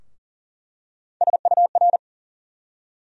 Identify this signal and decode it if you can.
Morse code, heard as a keyed tone, about 40 wpm, 700 Hz. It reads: SUR
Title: USA Call Sign Postfixes - 3 Letters 40wpm